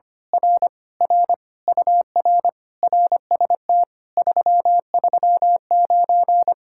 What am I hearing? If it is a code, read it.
Morse code, heard as a keyed tone, about 24 wpm, 700 Hz. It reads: R R UR RST 339
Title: Speed-Racing Common QSO Phrases - 25wpm